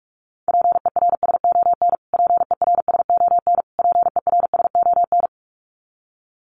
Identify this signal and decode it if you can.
Morse code, heard as a keyed tone, about 44 wpm, 700 Hz. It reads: PERSON PERSON PERSON
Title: Top 500 Words Repeated 45wpm